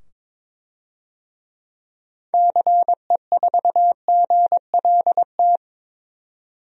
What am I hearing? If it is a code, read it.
Morse code, heard as a keyed tone, about 22 wpm, 700 Hz. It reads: CE4GLT
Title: DXCC-100 Call Signs - 6 Characters - 22wpm